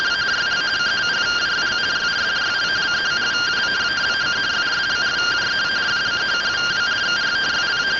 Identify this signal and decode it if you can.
Two-tone teleprinter signal, about 170 Hz shift.